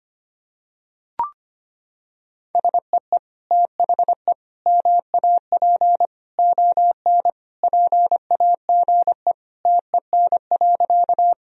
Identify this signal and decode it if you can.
Morse code, heard as a keyed tone, about 25 wpm, 700 Hz. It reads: SEE THE MAP ON PAGE TEN.
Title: Sentences from Top 1000 Words - Review Part 2 - 25wpm